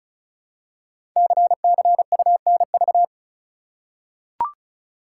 Morse code, heard as a keyed tone, about 36 wpm, 690 Hz. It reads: CCUNV
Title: Code Groups - 5 Letters - 35wpm